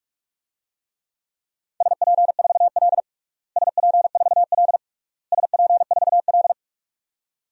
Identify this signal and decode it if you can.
Morse code, heard as a keyed tone, about 41 wpm, 700 Hz. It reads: SP4L SP4L SP4L
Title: DXCC-100 Call Signs - 4 Characters - Repeated 3x - 45wpm